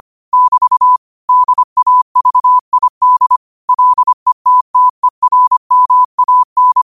Morse code, heard as a keyed tone, about 25 wpm, 1000 Hz. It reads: <BT> DAVID LETTERMAN
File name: 25wpm